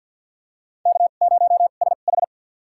Morse code, heard as a keyed tone, about 53 wpm, spaced as for 37 wpm, 700 Hz. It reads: K0SH
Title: USA Call Signs - 4 Characters - 2x Character Spacing - 50wpm